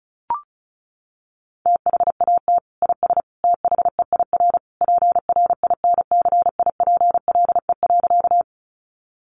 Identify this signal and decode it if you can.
Morse code, heard as a keyed tone, about 34 wpm, 700 Hz. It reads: THAT IS THEIR PRINCIPLE.
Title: Sentences from Top 1400 Words 35wpm